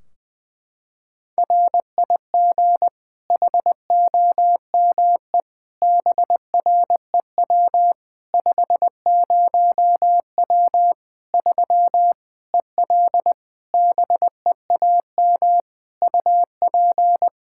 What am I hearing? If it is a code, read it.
Morse code, heard as a keyed tone, about 20 wpm, 700 Hz. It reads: RIG HOME BREW 50W 3 EL BEAM UP
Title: Practice QSO - Exchange 2 - Info - 20wpm